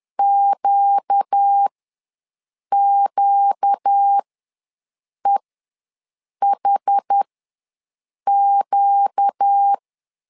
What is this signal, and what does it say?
Morse code, heard as a keyed tone, about 10 wpm, spaced as for 6 wpm, 800 Hz. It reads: QQEHQ
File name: LEZIONI / 4 Q ws 10wpm 2306